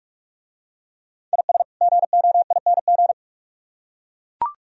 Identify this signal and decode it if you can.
Morse code, heard as a keyed tone, about 43 wpm, 700 Hz. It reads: IS GOING